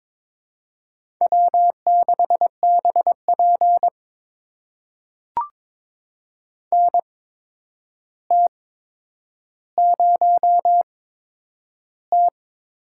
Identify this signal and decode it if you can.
Morse code, heard as a keyed tone, about 22 wpm, spaced as for 9 wpm, 700 Hz. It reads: W6BP NT0T
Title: USA Call Signs - 4 Characters - 8x Character Spacing - 22wpm